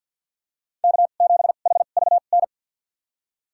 Morse code, heard as a keyed tone, about 53 wpm, spaced as for 37 wpm, 700 Hz. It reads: K7HVN
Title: USA Call Signs - 5 Characters - 2x Character Spacing - 50wpm